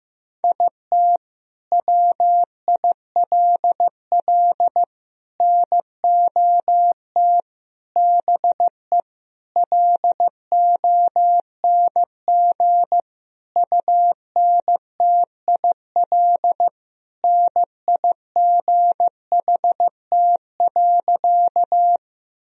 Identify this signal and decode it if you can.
Morse code, heard as a keyed tone, about 15 wpm, 700 Hz. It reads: IT WILL NOT BE LONG UNTIL NIGHT.
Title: Sentences from Top 1000 Words - Review Part 1 - 15wpm